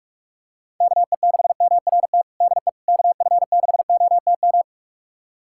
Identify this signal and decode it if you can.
Morse code, heard as a keyed tone, about 41 wpm, 700 Hz. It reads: KE6MRT DE KF6OTM
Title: USA Callsign DE Callsign - 6 Characters - 45wpm